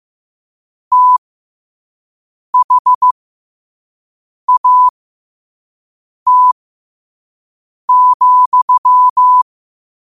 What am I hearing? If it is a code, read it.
Morse code, heard as a keyed tone, about 15 wpm, spaced as for 5 wpm, 1000 Hz. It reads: THAT,